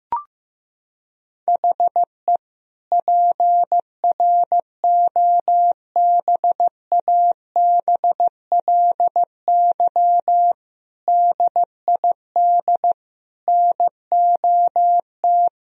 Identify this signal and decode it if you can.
Morse code, heard as a keyed tone, about 15 wpm, 700 Hz. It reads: HE PROBABLY DID NOT
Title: Sentences from Top 1000 Words - Review Part 4 - 15wpm